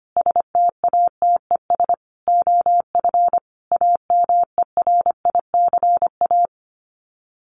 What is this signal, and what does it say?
Morse code, heard as a keyed tone, about 25 wpm, 700 Hz. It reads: STATES OF AMERICA
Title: Sets of 5 Words 25wpm